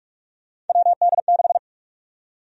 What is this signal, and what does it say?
Morse code, heard as a keyed tone, about 41 wpm, 700 Hz. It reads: WD6
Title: USA Call Sign Prefixes - Repeated 2x - 45wpm